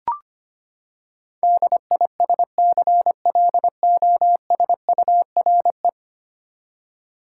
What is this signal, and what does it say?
Morse code, heard as a keyed tone, about 25 wpm, 700 Hz. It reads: DISCLOSURE
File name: Ten Letter Words - 25wpm